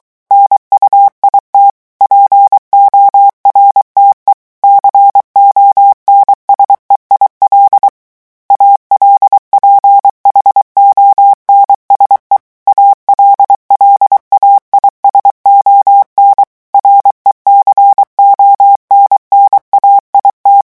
Morse code, heard as a keyed tone, about 23 wpm, 800 Hz. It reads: NUIT PORTE CONSEIL ALPHONSE ALLAIS ON RECONNAIT
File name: morse mp3 / revision9 citations humoristiques vitesse 22 mots minute